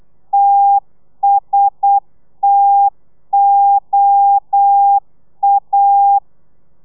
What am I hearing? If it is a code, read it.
Morse code, heard as a keyed tone, about 8 wpm, 800 Hz. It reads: TSTOA